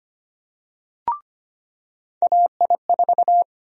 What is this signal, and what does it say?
Morse code, heard as a keyed tone, about 25 wpm, 700 Hz. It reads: AI4
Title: USA Call Sign Prefixes - 25wpm